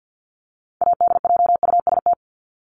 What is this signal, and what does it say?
Morse code, heard as a keyed tone, about 50 wpm, 700 Hz. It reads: ADJUST
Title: Top 1501-1750 Words 50wpm